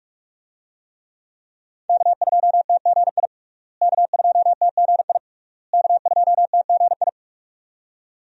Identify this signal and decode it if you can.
Morse code, heard as a keyed tone, about 46 wpm, 700 Hz. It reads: K2TGI K2TGI K2TGI
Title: USA Call Signs - 5 Characters - Repeated 3x - 45wpm